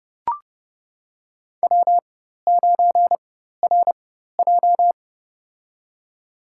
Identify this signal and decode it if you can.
Morse code, heard as a keyed tone, about 32 wpm, spaced as for 14 wpm, 700 Hz. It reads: W9RJ
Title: USA Call Signs - 4 Characters - 4x Character Spacing - 30wpm